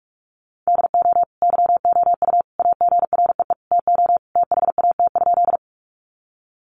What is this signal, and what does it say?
Morse code, heard as a keyed tone, about 44 wpm, 700 Hz. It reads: DO YOU AGREE TO THAT?
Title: Sentences from Top 1100 Words 45wpm